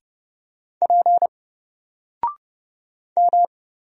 Morse code, heard as a keyed tone, about 30 wpm, 700 Hz. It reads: P M
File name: Single-Letter-Number-30wpm